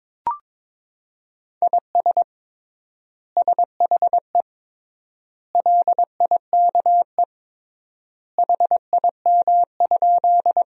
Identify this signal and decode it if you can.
Morse code, heard as a keyed tone, about 22 wpm, 700 Hz. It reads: IS SHE LIKE HIM?
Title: 3x Word-Spacing - Sentences from Top 100 Words - 22wpm